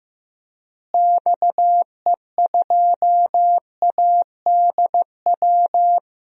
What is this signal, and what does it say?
Morse code, heard as a keyed tone, about 16 wpm, 700 Hz. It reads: XE2ADW
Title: DXCC-100 Call Signs - 6 Characters - Repeated 3x - 15wpm